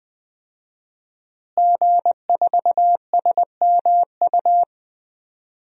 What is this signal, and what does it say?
Morse code, heard as a keyed tone, about 21 wpm, 700 Hz. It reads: G4SMU
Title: DXCC-100 Call Signs - 5 Characters - Repeated 3x - 20wpm